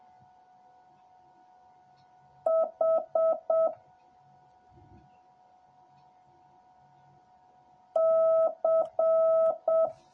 Morse code, heard as a keyed tone, about 7 wpm, 650 Hz. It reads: H C